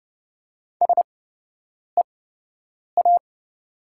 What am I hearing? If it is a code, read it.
Morse code, heard as a keyed tone, about 30 wpm, spaced as for 8 wpm, 700 Hz. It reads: SEA
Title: 8x Character Spacing - Top 300 Words - 30wpm